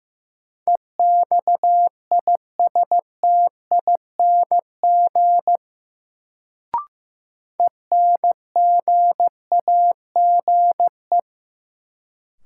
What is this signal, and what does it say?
Morse code, heard as a keyed tone, about 15 wpm, 700 Hz. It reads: EXISTING ENGAGE
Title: Sentences from Top 1500 Words - New Word Drill Alt - 15wpm